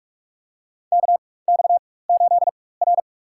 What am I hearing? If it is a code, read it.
Morse code, heard as a keyed tone, about 46 wpm, spaced as for 21 wpm, 700 Hz. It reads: KX8R
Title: USA Call Signs - 4 Characters - 4x Character Spacing - 45wpm